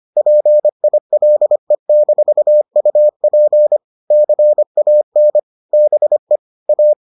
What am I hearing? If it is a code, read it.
Morse code, heard as a keyed tone, about 25 wpm, 600 Hz. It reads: PILE-UP CAN BE A